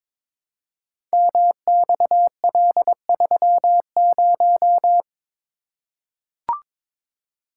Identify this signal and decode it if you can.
Morse code, heard as a keyed tone, about 23 wpm, 700 Hz. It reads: MXL30
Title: Code Groups - 5 Letter:Numbers - 22wpm